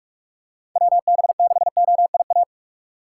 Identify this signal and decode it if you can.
Morse code, heard as a keyed tone, about 46 wpm, 700 Hz. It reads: WB6OIA
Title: USA Call Signs - 8x Character Spacing - 45wpm